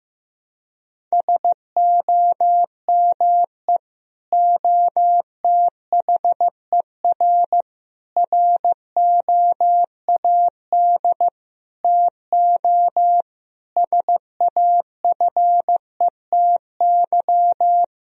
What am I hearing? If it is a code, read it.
Morse code, heard as a keyed tone, about 15 wpm, 700 Hz. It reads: SOME OTHER ROAD TO SAFETY
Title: Sentences from Top 1000 Words - Review Part 4 - 15wpm